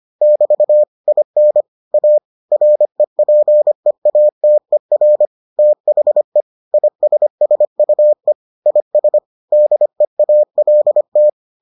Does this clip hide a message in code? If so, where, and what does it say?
Yes: Morse code from 0.2 to 11.3 s, heard as a keyed tone, about 25 wpm, 600 Hz. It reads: <BT> IN A REPEATER THE ISSUE IS DEALT